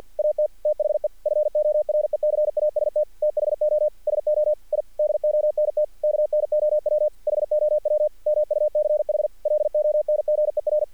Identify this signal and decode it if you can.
Morse code, heard as a keyed tone, about 50 wpm, 600 Hz. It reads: AT THE FORECAST THO SO I DONT KNOW HOW MUCH LONGER
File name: archived / CRSnet